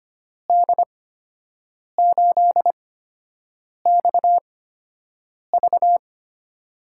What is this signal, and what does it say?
Morse code, heard as a keyed tone, about 25 wpm, spaced as for 7 wpm, 700 Hz. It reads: D8XV
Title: USA Call Signs - 5 Characters - 8x Character Spacing - 25wpm